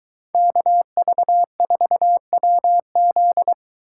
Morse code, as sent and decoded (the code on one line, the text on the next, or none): -.- ...- ....- .-- --..
KV4WZ